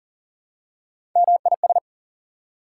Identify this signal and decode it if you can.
Morse code, heard as a keyed tone, about 40 wpm, 700 Hz. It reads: MIS
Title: Prefixes and Suffixes 40wpm